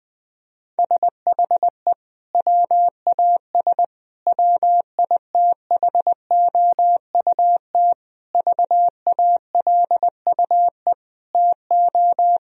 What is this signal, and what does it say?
Morse code, heard as a keyed tone, about 20 wpm, 700 Hz. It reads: SHE WAS WITHOUT VALUE TO
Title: Sentences from Top 1000 Words - Review Part 3 - 20wpm